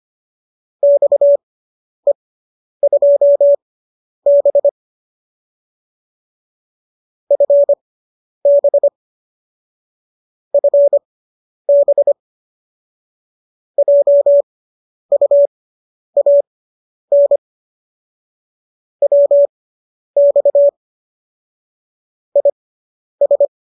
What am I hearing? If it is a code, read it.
Morse code, heard as a keyed tone, about 25 wpm, spaced as for 10 wpm, 570 Hz. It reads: XE2B FB FB JUAN WX IS